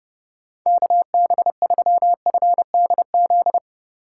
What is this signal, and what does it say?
Morse code, heard as a keyed tone, about 30 wpm, 700 Hz. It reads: KB3FDZ
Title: USA Call Signs - 6 Characters - 2x Character Spacing - 30wpm